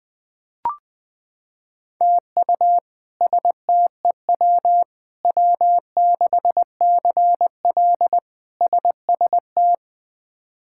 Morse code, heard as a keyed tone, about 20 wpm, 700 Hz. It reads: TU STEW W6CL SST
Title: K1USN SST QSO Practice - 20wpm